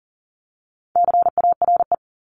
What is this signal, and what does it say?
Morse code, heard as a keyed tone, about 39 wpm, 700 Hz. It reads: CARE
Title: Multi-letter Phonemes 40wpm